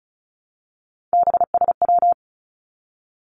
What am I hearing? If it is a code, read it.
Morse code, heard as a keyed tone, about 36 wpm, 700 Hz. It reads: BSW
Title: Three Challenging Letter-Number Combinations 35wpm